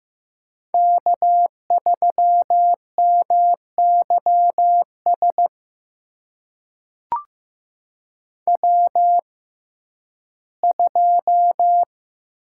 Morse code, heard as a keyed tone, about 15 wpm, spaced as for 11 wpm, 700 Hz. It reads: K3MYS W2
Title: USA Call Signs - 5 Characters - 6x Character Spacing - 15wpm